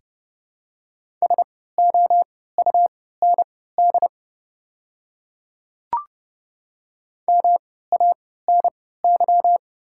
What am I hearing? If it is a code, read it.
Morse code, heard as a keyed tone, about 30 wpm, spaced as for 17 wpm, 700 Hz. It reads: SOUND MANY
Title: 3x Character Spacing - Top 100 Words - 30wpm